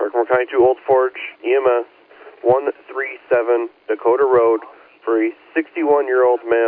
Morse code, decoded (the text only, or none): none